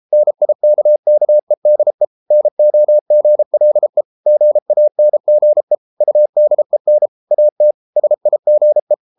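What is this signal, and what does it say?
Morse code, heard as a keyed tone, about 33 wpm, 600 Hz. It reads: NIKKEDE NOGLE GANGE UDEN AT SIGE
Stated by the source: Morse code at 33 wpm, QRQ.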